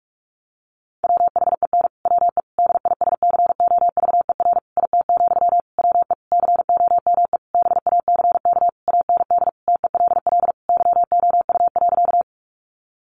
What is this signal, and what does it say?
Morse code, heard as a keyed tone, about 44 wpm, 700 Hz. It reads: WHEN WE DISCOVER IT, WE COME BACK AND TELL YOU.
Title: Sentences from Top 1100 Words 45wpm